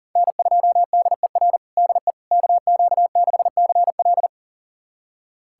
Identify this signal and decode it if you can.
Morse code, heard as a keyed tone, about 40 wpm, 700 Hz. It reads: N2DER DE KQ6CL
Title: USA Callsign DE Callsign - 5 Characters - 40wpm